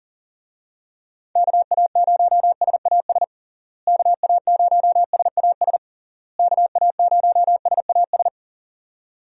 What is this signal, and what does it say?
Morse code, heard as a keyed tone, about 40 wpm, 700 Hz. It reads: KA0SAS KA0SAS KA0SAS
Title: USA Call Signs - 6 Characters - Repeated 3x - 40wpm